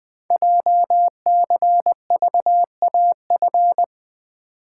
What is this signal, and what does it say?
Morse code, heard as a keyed tone, about 20 wpm, 700 Hz. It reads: JCVAF
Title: Code Groups - 5 Letters - 20wpm